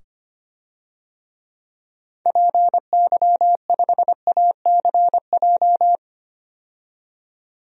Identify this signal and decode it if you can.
Morse code, heard as a keyed tone, about 25 wpm, 700 Hz. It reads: PY5ACJ
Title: DXCC-100 Call Signs - 6 Characters - 25wpm